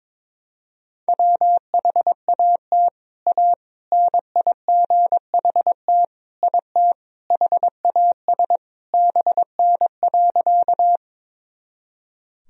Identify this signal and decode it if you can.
Morse code, heard as a keyed tone, about 22 wpm, 700 Hz. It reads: WHAT A NIGHT IT HAS BN.
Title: Sentences from Top 300 Words and Abbreviations 22wpm